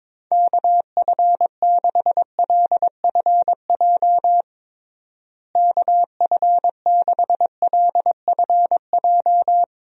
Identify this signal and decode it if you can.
Morse code, heard as a keyed tone, about 21 wpm, 700 Hz. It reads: KF6LFJ KF6LFJ
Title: USA Call Signs - 6 Characters - Repeated 2x - 22wpm